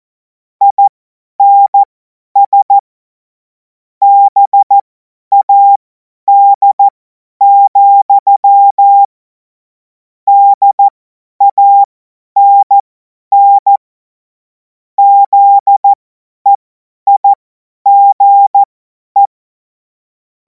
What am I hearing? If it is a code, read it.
Morse code, heard as a keyed tone, about 14 wpm, spaced as for 10 wpm, 800 Hz. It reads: INS BAD, DANN ZEIGE
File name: Morsekurs / Witze / Witz 24. Februar 2026 50 F70